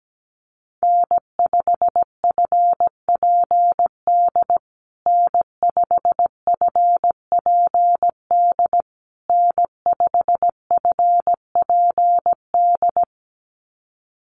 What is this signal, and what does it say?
Morse code, heard as a keyed tone, about 17 wpm, 700 Hz. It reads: N5FPD N5FPD N5FPD
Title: USA Call Signs Repeated 3x 17wpm